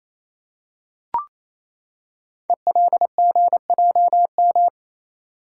Morse code, heard as a keyed tone, about 27 wpm, 700 Hz. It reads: ELGJM
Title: Code Groups - 5 Letters - 28wpm